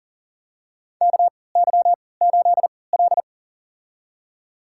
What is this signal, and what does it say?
Morse code, heard as a keyed tone, about 40 wpm, spaced as for 23 wpm, 700 Hz. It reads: KY8L
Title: USA Call Signs - 4 Characters - 3x Character Spacing - 40wpm